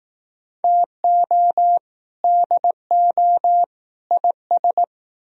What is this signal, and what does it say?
Morse code, heard as a keyed tone, about 18 wpm, 700 Hz. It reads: TO DO IS
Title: Speed-Racing - Sets of 3 Words 15wpm